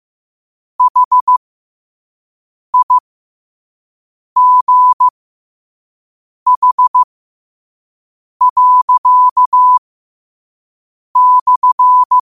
Morse code, heard as a keyed tone, about 15 wpm, spaced as for 5 wpm, 1000 Hz. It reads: HIGH./
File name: qotdfiles / 5wpm